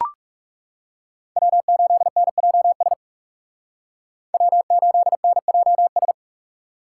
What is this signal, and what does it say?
Morse code, heard as a keyed tone, about 38 wpm, 700 Hz. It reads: W8NJS W8NJS